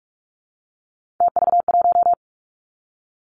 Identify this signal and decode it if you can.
Morse code, heard as a keyed tone, about 46 wpm, 700 Hz. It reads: TV1
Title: Three Challenging Letter-Number Combinations 45wpm